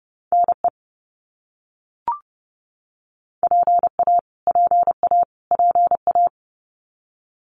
Morse code, heard as a keyed tone, about 30 wpm, 700 Hz. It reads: NE PA PA PA
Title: US States Repeated 3x 30wpm